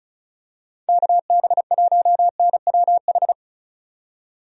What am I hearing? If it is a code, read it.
Morse code, heard as a keyed tone, about 34 wpm, 700 Hz. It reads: KB1NWH
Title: USA Call Signs - 6 Characters - 35wpm